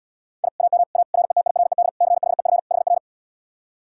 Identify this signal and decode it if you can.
Morse code, heard as a keyed tone, about 78 wpm, 700 Hz. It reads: I AM A RETIRED CHEF HR